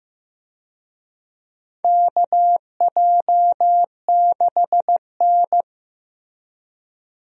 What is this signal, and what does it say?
Morse code, heard as a keyed tone, about 15 wpm, 700 Hz. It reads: KJ6N
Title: USA Call Signs - 4 Characters - Repeated 3x - 15wpm